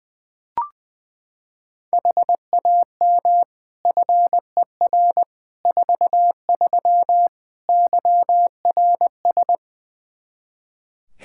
Morse code, heard as a keyed tone, about 20 wpm, 700 Hz. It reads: HAM FER 43 YRS
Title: Practice QSO - Exchange 3 - More Info - 20wpm